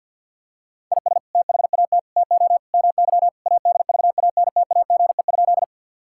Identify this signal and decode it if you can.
Morse code, heard as a keyed tone, about 51 wpm, 700 Hz. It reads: IS THAT TO MY ADVANTAGE?
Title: Sentences from Top 1000 Words - Review Part 6 - 50wpm